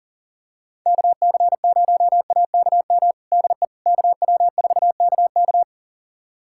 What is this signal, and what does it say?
Morse code, heard as a keyed tone, about 40 wpm, 700 Hz. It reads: KC0AKM DE KW4KK
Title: USA Callsign DE Callsign 40wpm